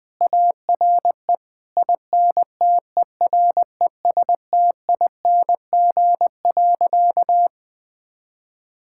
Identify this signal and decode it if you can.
Morse code, heard as a keyed tone, about 20 wpm, 700 Hz. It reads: ARE INTERESTING.
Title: Sentences with 2-Word Phrases - 20wpm